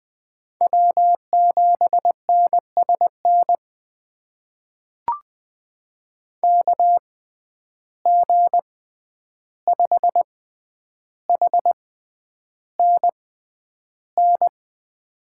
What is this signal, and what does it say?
Morse code, heard as a keyed tone, about 20 wpm, spaced as for 10 wpm, 700 Hz. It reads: W7NSN KG5HNN
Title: USA Call Signs - 6x Character Spacing - 20wpm